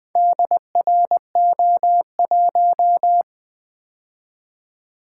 Morse code, heard as a keyed tone, about 23 wpm, 700 Hz. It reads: DRO1
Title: Speed-Racing - Four Letter-Number Combinations - 20wpm